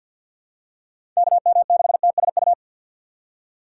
Morse code, heard as a keyed tone, about 52 wpm, 700 Hz. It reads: KM6TSU
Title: USA Call Signs - 6 Characters - 3x Character Spacing - 50wpm